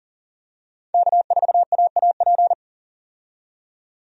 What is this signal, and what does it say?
Morse code, heard as a keyed tone, about 40 wpm, 700 Hz. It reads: K4AAP
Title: USA Call Signs - 5 Characters - 40wpm